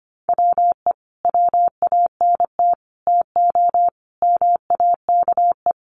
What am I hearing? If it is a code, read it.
Morse code, heard as a keyed tone, about 25 wpm, 700 Hz. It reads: WE WANT TO MAKE
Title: Sets of 5 Words Encore2 25wpm